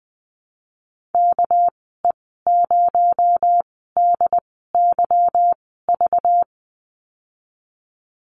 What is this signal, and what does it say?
Morse code, heard as a keyed tone, about 20 wpm, spaced as for 15 wpm, 700 Hz. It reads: KE0DYV
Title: 219 Lesson 35g - Number 0 - Call Signs 20wpm